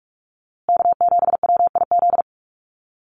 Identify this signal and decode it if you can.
Morse code, heard as a keyed tone, about 44 wpm, 700 Hz. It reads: K7WIZ
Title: USA Call Signs Repeated 2x 45wpm